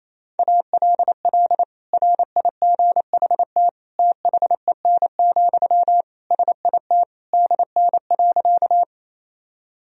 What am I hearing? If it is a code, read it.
Morse code, heard as a keyed tone, about 28 wpm, 700 Hz. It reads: ALL RIGHT THEN, SIT DN.
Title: Sentences from Top 500 Words and Abbreviations 28wpm